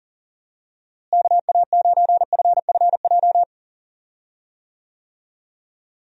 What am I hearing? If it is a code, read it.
Morse code, heard as a keyed tone, about 37 wpm, 700 Hz. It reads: KA9FFJ
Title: USA Call Signs - 6 Characters - Repeated 3x - 40wpm